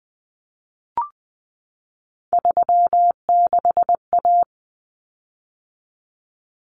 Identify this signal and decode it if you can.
Morse code, heard as a keyed tone, about 20 wpm, 700 Hz. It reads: 36A
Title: Common Cut Numbers - 3-Digit 20wpm